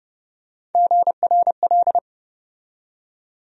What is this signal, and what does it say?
Morse code, heard as a keyed tone, about 32 wpm, 700 Hz. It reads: GRL
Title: USA Call Sign Postfixes - 3 Letters 30wpm